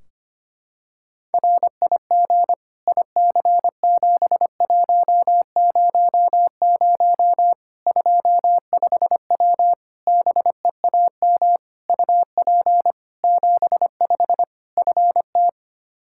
Morse code, heard as a keyed tone, about 25 wpm, 700 Hz. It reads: RIG IC7100 25W BEAM UP 75 FT
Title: Practice QSO - Exchange 2 - Info - 25wpm